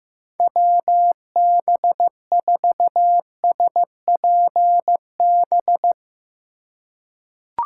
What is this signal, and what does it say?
Morse code, heard as a keyed tone, about 15 wpm, 700 Hz. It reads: WB4SPB
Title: K1USN SST QSO Practice - 15wpm